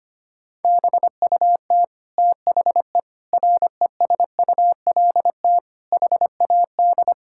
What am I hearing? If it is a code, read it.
Morse code, heard as a keyed tone, about 25 wpm, 700 Hz. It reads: BUT THE RESULT HAD
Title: Sentences from Top 1000 Words - Review Part 5 - 25wpm